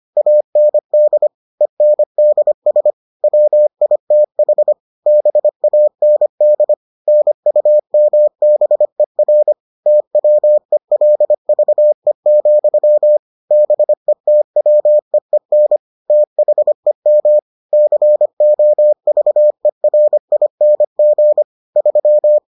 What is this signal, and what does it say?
Morse code, heard as a keyed tone, about 25 wpm, 600 Hz. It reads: AND ENDS WITH BAND NUMBER TWELVE, BETWEEN THEM COVERING 3